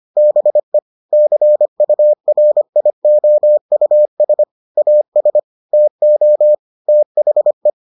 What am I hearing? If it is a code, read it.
Morse code, heard as a keyed tone, about 25 wpm, 600 Hz. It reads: BE CURIOUS AS TO THE